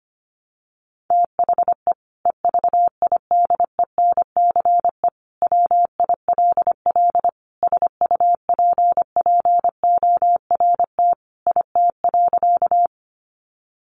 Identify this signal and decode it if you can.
Morse code, heard as a keyed tone, about 25 wpm, 700 Hz. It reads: THE EVIDENCE WILL SUPPORT IT.
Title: Sentences from Top 1300 Words 25wpm